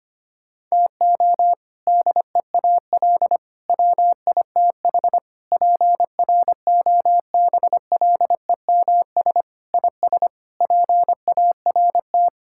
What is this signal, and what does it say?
Morse code, heard as a keyed tone, about 25 wpm, 700 Hz. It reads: TO DEAL WITH PROBLEMS IS PART
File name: practice / 25wpm / Sentences with 3-Word Phrases - 25wpm